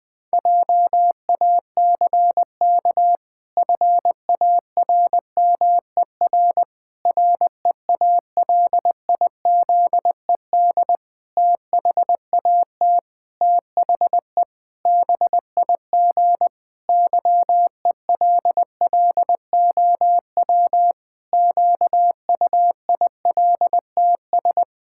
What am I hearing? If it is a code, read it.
Morse code, heard as a keyed tone, about 20 wpm, 700 Hz. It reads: JACK FARMER REALIZED THAT THE BIG YELLOW QUILTS
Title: Pangram Challenge v2 - 20wpm